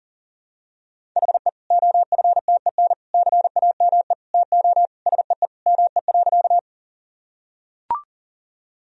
Morse code, heard as a keyed tone, about 40 wpm, 700 Hz. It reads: HE OFTEN CAME TO SEE ME.
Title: Sentences from Top 1000 Words - Review Part 1 - 40wpm